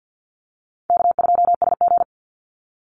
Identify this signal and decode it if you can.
Morse code, heard as a keyed tone, about 48 wpm, 700 Hz. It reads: K2SG
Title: Easy Call Signs Repeated 3x - 50wpm